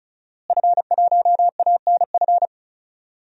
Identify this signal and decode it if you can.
Morse code, heard as a keyed tone, about 36 wpm, 700 Hz. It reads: F1ANF
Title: DXCC-100 Call Signs - 5 Characters - Repeated 3x - 35wpm